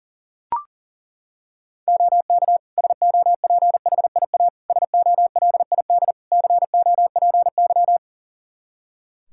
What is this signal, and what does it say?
Morse code, heard as a keyed tone, about 40 wpm, 700 Hz. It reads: OK SOPHIA SOLID COPY
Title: Practice QSO - Exchange 3 - More Info - 40wpm